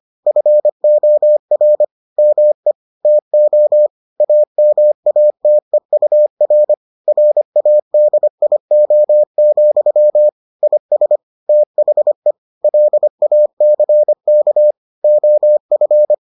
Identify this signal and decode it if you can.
Morse code, heard as a keyed tone, about 25 wpm, 600 Hz. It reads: FOR ME TO AMATEUR RADIO, IS THE LACK OF